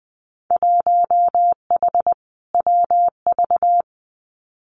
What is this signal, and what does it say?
Morse code, heard as a keyed tone, about 19 wpm, 700 Hz. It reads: 1H WV
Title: Winter Field Day Exchanges 20wpm